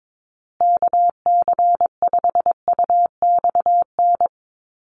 Morse code, heard as a keyed tone, about 22 wpm, 700 Hz. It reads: KC5UXN
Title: USA Call Signs Repeated 2x 22wpm